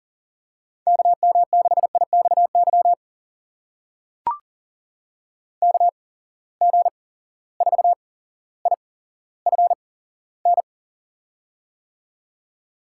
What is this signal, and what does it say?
Morse code, heard as a keyed tone, about 40 wpm, spaced as for 17 wpm, 700 Hz. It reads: KM6IXY KG4IFN
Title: USA Call Signs - 6 Characters - 8x Character Spacing - 40wpm